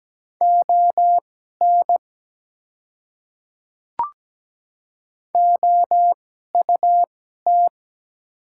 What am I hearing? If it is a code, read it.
Morse code, heard as a keyed tone, about 17 wpm, spaced as for 12 wpm, 700 Hz. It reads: ON OUT
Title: 2x Character Spacing - Top 100 Words - 17wpm